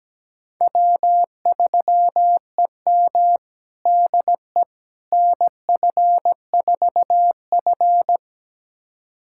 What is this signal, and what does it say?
Morse code, heard as a keyed tone, about 17 wpm, 700 Hz. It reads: W3EM DE NF4F
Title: USA Callsign DE Callsign - 4 Characters - 17wpm